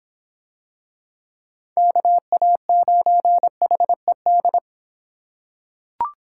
Morse code, heard as a keyed tone, about 24 wpm, 700 Hz. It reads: KA9HED